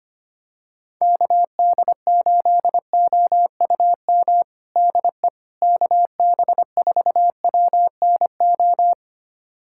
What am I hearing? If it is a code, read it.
Morse code, heard as a keyed tone, about 25 wpm, 700 Hz. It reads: KD8OUM DE KB4WNO
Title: USA Callsign DE Callsign 25wpm